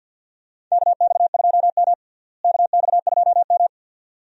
Morse code, heard as a keyed tone, about 51 wpm, 700 Hz. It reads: KX2M KX2M
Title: USA Call Signs - 4 Characters - Repeated 2x - 50wpm